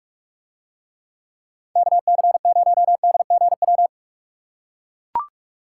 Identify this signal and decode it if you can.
Morse code, heard as a keyed tone, about 41 wpm, 700 Hz. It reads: KC0DGW